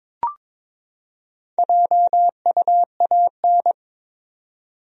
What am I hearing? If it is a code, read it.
Morse code, heard as a keyed tone, about 21 wpm, 700 Hz. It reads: JUAN
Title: USA Common Names - 22wpm